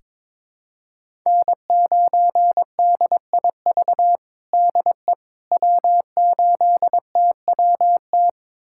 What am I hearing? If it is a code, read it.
Morse code, heard as a keyed tone, about 22 wpm, 700 Hz. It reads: N9DIV DE W8TWT
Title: USA Callsign DE Callsign - 5 Characters - 22wpm